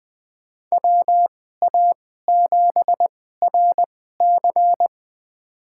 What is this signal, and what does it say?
Morse code, heard as a keyed tone, about 20 wpm, spaced as for 14 wpm, 700 Hz. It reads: WA7RC
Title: USA Call Signs - 2x Character Spacing - 20wpm